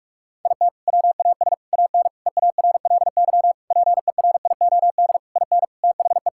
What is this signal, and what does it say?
Morse code, heard as a keyed tone, about 46 wpm, 700 Hz. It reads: IT WAS AN EARLY PERIOD IN THE
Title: Sentences from Top 1000 Words - Review Part 5 - 45wpm